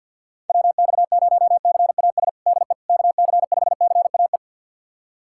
Morse code, heard as a keyed tone, about 53 wpm, 700 Hz. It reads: WX0CAS DE KC5CAE
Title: USA Callsign DE Callsign - 6 Characters - 50wpm